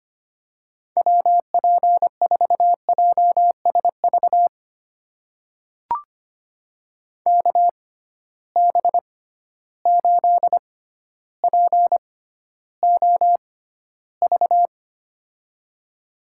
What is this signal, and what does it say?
Morse code, heard as a keyed tone, about 25 wpm, spaced as for 13 wpm, 700 Hz. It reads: WP4JSV KB8POV
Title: USA Call Signs - 6 Characters - 6x Character Spacing - 25wpm